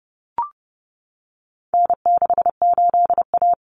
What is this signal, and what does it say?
Morse code, heard as a keyed tone, about 30 wpm, 700 Hz. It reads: N68A
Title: Common Cut Numbers - 4-Digit 30wpm